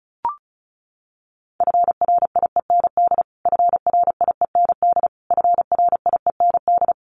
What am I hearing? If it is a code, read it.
Morse code, heard as a keyed tone, about 34 wpm, 700 Hz. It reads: FRIEND FRIEND FRIEND
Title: Top 300 Words Repeated 35wpm